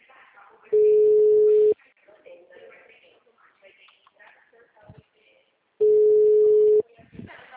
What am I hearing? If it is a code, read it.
Ringback tone (Europe, 425 Hz), with speech.
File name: out- -2030- - - .40091